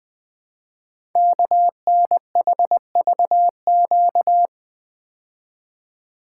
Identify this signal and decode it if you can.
Morse code, heard as a keyed tone, about 21 wpm, 700 Hz. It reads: KNHVQ
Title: Code Groups - 5 Letters - 20wpm